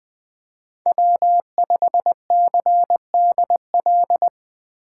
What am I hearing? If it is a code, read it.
Morse code, heard as a keyed tone, about 20 wpm, 700 Hz. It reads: W5CDL
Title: USA Call Signs - 5 Characters - 2x Character Spacing - 20wpm